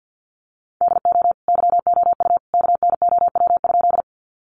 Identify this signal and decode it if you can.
Morse code, heard as a keyed tone, about 50 wpm, 700 Hz. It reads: DO YOU KNOW?
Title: Sentences from Top 100 Words 50wpm